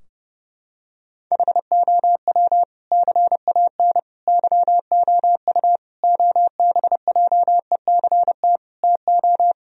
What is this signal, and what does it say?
Morse code, heard as a keyed tone, about 30 wpm, 700 Hz. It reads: HOW CAN YOU OBJECT TO
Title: Sentences from Top 1000 Words - Review Part 3 - 30wpm